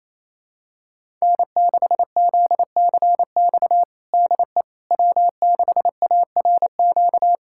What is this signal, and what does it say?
Morse code, heard as a keyed tone, about 28 wpm, 700 Hz. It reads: N6ZCX DE W6ARQ
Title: USA Callsign DE Callsign - 5 Characters - 28wpm